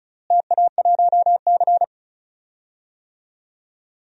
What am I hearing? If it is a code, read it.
Morse code, heard as a keyed tone, about 36 wpm, 700 Hz. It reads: TA1C
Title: DXCC-100 Call Signs - 4 Characters - Repeated 3x - 35wpm